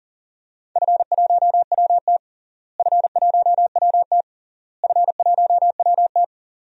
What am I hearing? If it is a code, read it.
Morse code, heard as a keyed tone, about 40 wpm, 700 Hz. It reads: F1WT F1WT F1WT
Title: DXCC-100 Call Signs - 4 Characters - Repeated 3x - 40wpm